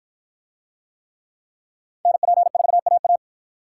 Morse code, heard as a keyed tone, about 53 wpm, 700 Hz. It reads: NP4AA